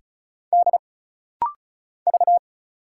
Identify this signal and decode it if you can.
Morse code, heard as a keyed tone, about 35 wpm, 700 Hz. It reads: D V
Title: Single Letter-Number - Rapid-Fire 35wpm